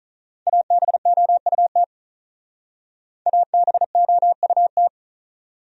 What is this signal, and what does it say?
Morse code, heard as a keyed tone, about 44 wpm, 700 Hz. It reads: ABOUT ABOUT
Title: Speed-Racing Top 100 Words 35wpm